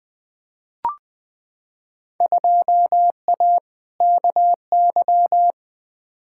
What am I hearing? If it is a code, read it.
Morse code, heard as a keyed tone, about 20 wpm, 700 Hz. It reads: 2A KY
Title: Field Day Exchanges - 20wpm